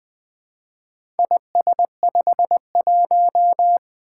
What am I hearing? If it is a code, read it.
Morse code, heard as a keyed tone, about 20 wpm, 700 Hz. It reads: IS51
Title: Four Letter-Number Combinations - 20wpm